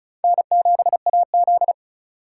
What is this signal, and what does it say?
Morse code, heard as a keyed tone, about 36 wpm, 700 Hz. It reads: N7AZ
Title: K1USN SST QSO Practice - 35wpm